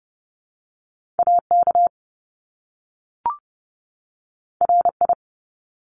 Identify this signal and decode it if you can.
Morse code, heard as a keyed tone, about 30 wpm, 700 Hz. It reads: AK RI
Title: ARRL Sections 30wpm